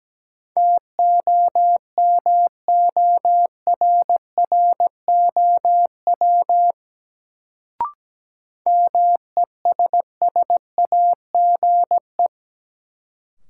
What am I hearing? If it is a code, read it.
Morse code, heard as a keyed tone, about 17 wpm, 700 Hz. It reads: TOMORROW MESSAGE
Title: Sentences from Top 1100 Words - New Word Drill - 17wpm